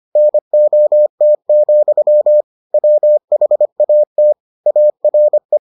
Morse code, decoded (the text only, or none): NOT, WHAT ARE